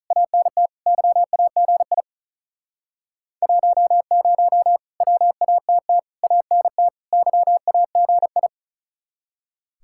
Morse code, heard as a keyed tone, about 46 wpm, 700 Hz. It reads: ANT YAGI 10 WATT ANT YAGI